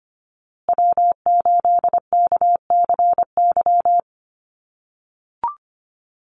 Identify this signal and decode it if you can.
Morse code, heard as a keyed tone, about 25 wpm, 700 Hz. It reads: W8KCY
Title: USA Call Signs Repeated 2x 25wpm